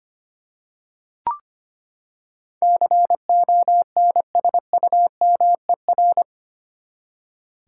Morse code, heard as a keyed tone, about 25 wpm, 700 Hz. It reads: CONSUMER
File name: Eight Letter Words - 25wpm